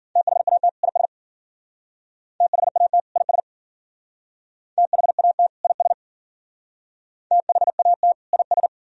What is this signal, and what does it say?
Morse code, heard as a keyed tone, about 52 wpm, 700 Hz. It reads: THAT IS THAT IS THAT IS THAT IS